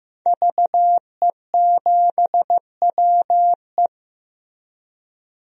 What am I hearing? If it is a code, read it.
Morse code, heard as a keyed tone, about 14 wpm, 700 Hz. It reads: VE7WE
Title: DXCC-100 Call Signs - 5 Characters - Repeated 3x - 15wpm